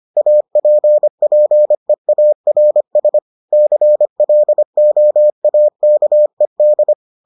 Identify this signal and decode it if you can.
Morse code, heard as a keyed tone, about 25 wpm, 600 Hz. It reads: APPEARS CLOAKED